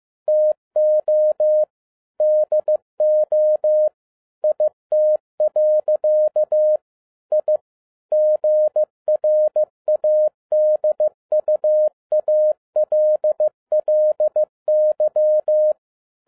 Morse code, heard as a keyed tone, about 15 wpm, 610 Hz. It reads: TO DO IT. I GRADUALLY